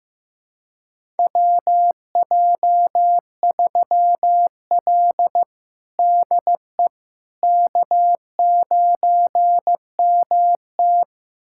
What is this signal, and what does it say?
Morse code, heard as a keyed tone, about 15 wpm, 700 Hz. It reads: WJ3L DE K9MT
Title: USA Callsign DE Callsign - 4 Characters - 15wpm